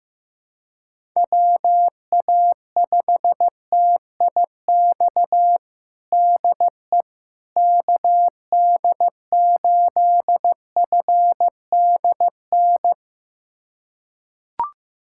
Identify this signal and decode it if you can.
Morse code, heard as a keyed tone, about 15 wpm, 700 Hz. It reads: WA5TIX DE KD8FDN
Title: USA Callsign DE Callsign - 6 Characters - 15wpm